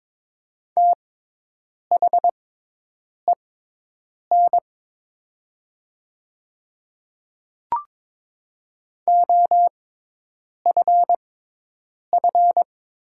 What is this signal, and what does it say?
Morse code, heard as a keyed tone, about 22 wpm, spaced as for 8 wpm, 700 Hz. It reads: THEN OFF
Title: 6x Character Spacing - Top 200 Words - 22wpm